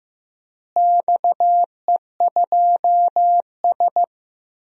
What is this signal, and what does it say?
Morse code, heard as a keyed tone, about 15 wpm, 700 Hz. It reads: XE2S
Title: DXCC-100 Call Signs - 4 Characters - Repeated 3x - 15wpm